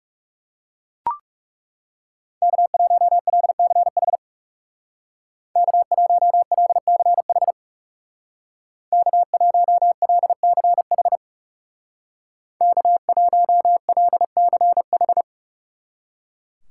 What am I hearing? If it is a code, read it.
Morse code, heard as a keyed tone, about 39 wpm, 700 Hz. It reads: K1LCH K1LCH K1LCH K1LCH